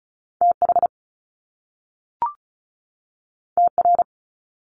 Morse code, heard as a keyed tone, about 36 wpm, 700 Hz. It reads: TH TR
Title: Top Two Letter Combinations in Words 35wpm